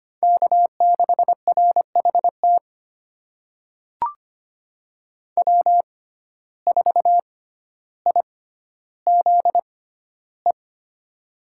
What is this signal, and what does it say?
Morse code, heard as a keyed tone, about 25 wpm, spaced as for 13 wpm, 700 Hz. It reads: K6RHT W4IZE
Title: USA Call Signs - 5 Characters - 6x Character Spacing - 25wpm